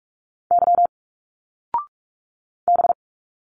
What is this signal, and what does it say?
Morse code, heard as a keyed tone, about 46 wpm, 700 Hz. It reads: Y B
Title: Single Letters 45wpm